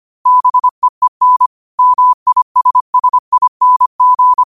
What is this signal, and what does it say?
Morse code, heard as a keyed tone, about 25 wpm, 1000 Hz. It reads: BEEN MISSING